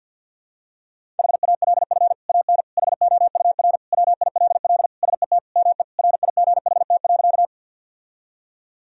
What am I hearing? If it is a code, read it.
Morse code, heard as a keyed tone, about 51 wpm, 700 Hz. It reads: HALF AN HOUR WILL SET ME RIGHT.
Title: Sentences from Top 1000 Words - Review Part 1 - 50wpm